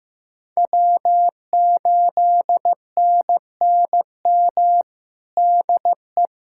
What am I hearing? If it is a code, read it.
Morse code, heard as a keyed tone, about 16 wpm, 700 Hz. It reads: W8NNM DE
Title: USA Callsign DE Callsign - 5 Characters - 15wpm